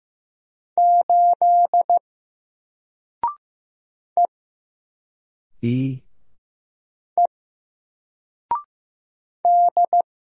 Morse code, heard as a keyed tone, about 15 wpm, 700 Hz. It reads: 8 E E D
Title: Single Letter-Number Alt 15wpm